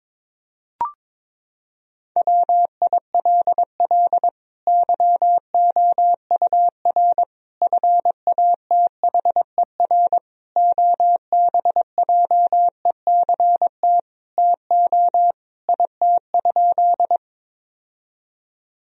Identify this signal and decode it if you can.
Morse code, heard as a keyed tone, about 22 wpm, 700 Hz. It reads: WILL YOUR FATHER OBJECT TO IT?
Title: Sentences from Top 1000 Words - Review Part 3 - 22wpm